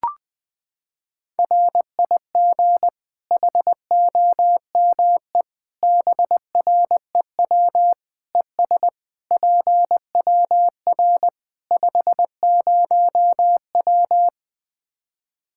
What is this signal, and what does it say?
Morse code, heard as a keyed tone, about 20 wpm, 700 Hz. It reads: RIG HOME BREW ES PWR 50W
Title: Practice QSO - Exchange 2 - Info - 20wpm